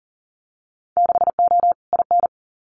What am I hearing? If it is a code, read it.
Morse code, heard as a keyed tone, about 40 wpm, 700 Hz. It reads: 6O IN
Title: Winter Field Day Exchanges Encore 40wpm